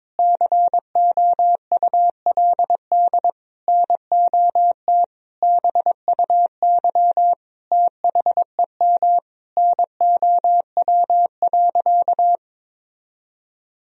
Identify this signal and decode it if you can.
Morse code, heard as a keyed tone, about 22 wpm, 700 Hz. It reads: COULD NOT BUY THEM NOW.
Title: Sentences from Top 1000 Words - Review Part 4 - 22wpm